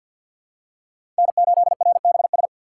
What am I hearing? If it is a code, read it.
Morse code, heard as a keyed tone, about 48 wpm, 700 Hz. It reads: N8RBS
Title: USA Call Signs - 5 Characters - 4x Character Spacing - 50wpm